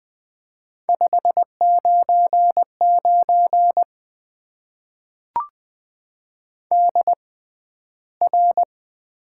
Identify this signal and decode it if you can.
Morse code, heard as a keyed tone, about 20 wpm, spaced as for 12 wpm, 700 Hz. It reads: 599 DR
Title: Top 90 QSO Elements - 6x Character Spacing - 20wpm